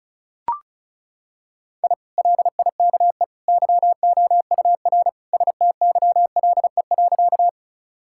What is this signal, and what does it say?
Morse code, heard as a keyed tone, about 36 wpm, 700 Hz. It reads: I LIKE YOUR STYLE.
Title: Sentences from Top 1000 Words - Review Part 5 - 35wpm